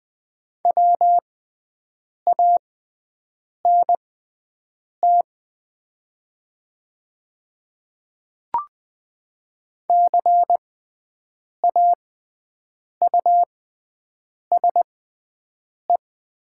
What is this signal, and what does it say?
Morse code, heard as a keyed tone, about 20 wpm, spaced as for 7 wpm, 700 Hz. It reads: WANT CAUSE
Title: 6x Character Spacing - Top 200 Words - 20wpm